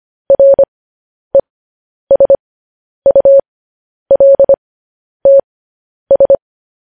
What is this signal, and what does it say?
Morse code, heard as a keyed tone, about 25 wpm, spaced as for 10 wpm, 550 Hz. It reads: RESULTS
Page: news-headlines-in-morse-code-at-10-wpm